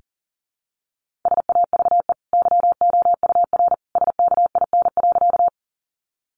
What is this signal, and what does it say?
Morse code, heard as a keyed tone, about 37 wpm, 700 Hz. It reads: SAVE YOUR SKIN.
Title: Sentences from Top 1300 Words 40wpm